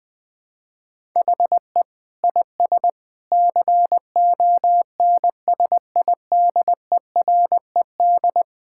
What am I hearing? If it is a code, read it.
Morse code, heard as a keyed tone, about 21 wpm, 700 Hz. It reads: HE IS CONSIDERED
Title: Sentences with 7-Word Phrases - 20wpm